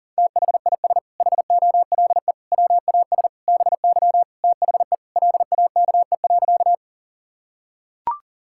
Morse code, heard as a keyed tone, about 40 wpm, 700 Hz. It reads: THIS HOLE WAS BY THE LAKE.
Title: Sentences from Top 1600 Words 40wpm